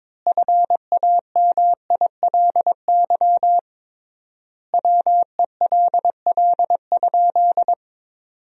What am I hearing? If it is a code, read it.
Morse code, heard as a keyed tone, about 22 wpm, 700 Hz. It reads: FAMILY WELL?
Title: 3x Word-Spacing - Sentences from Top 400 Words - 22wpm